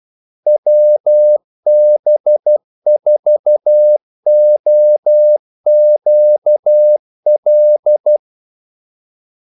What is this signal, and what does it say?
Morse code, heard as a keyed tone, about 12 wpm, 600 Hz. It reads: WB4OQL
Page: 2 ekstra ordmellemrum mellem kaldesignalerne